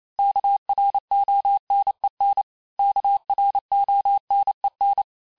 Morse code, heard as a keyed tone, about 28 wpm, 800 Hz. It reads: KRONEN KRONEN
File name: cours-du-mercredi / audios / cours-865-teste-les-mots-vitesse-27mpm